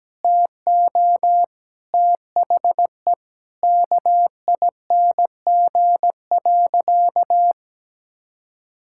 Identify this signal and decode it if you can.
Morse code, heard as a keyed tone, about 17 wpm, 700 Hz. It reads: TO THE KING.
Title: Sentences from Top 1000 Words - Review Part 6 - 17wpm